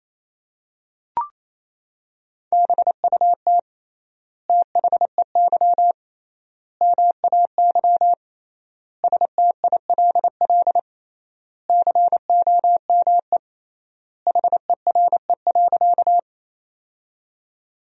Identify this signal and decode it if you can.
Morse code, heard as a keyed tone, about 28 wpm, 700 Hz. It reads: BUT THEY MAY STILL COME HERE.
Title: 3x Word-Spacing - Sentences from Top 300 Words - 28wpm